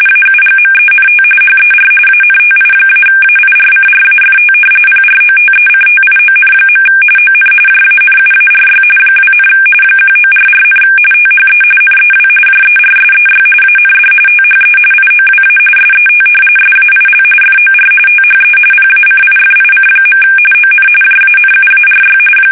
RTTY, about 45.45 baud. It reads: AUGMENT HER ANTI-AIRCRAFT PROTECTION. ON JANUARY 2, 1944, THE NEW JERSEY AND THE IOWA SET SAIL FOR THE PACIFIC. ON JANUARY
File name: fsk850 / battleshipnj